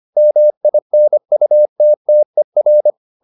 Morse code, heard as a keyed tone, about 25 wpm, 600 Hz. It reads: MINUTTER
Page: Ferie 25 wpm